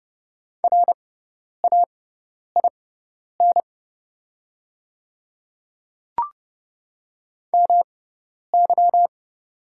Morse code, heard as a keyed tone, about 30 wpm, spaced as for 10 wpm, 700 Hz. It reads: RAIN MY
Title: 6x Character Spacing - Top 300 Words - 30wpm